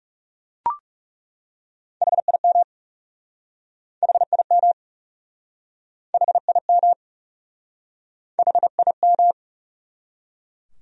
Morse code, heard as a keyed tone, about 40 wpm, 700 Hz. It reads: HIM HIM HIM HIM